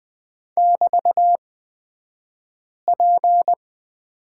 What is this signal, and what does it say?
Morse code, heard as a keyed tone, about 20 wpm, spaced as for 5 wpm, 700 Hz. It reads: <BT>P
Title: 223 Lesson 36d - BT - Two Characters 20wpm